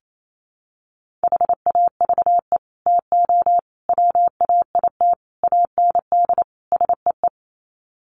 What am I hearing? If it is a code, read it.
Morse code, heard as a keyed tone, about 28 wpm, 700 Hz. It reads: HAVE TO WAIT AND SEE
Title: Sets of 5 Words Encore2 28wpm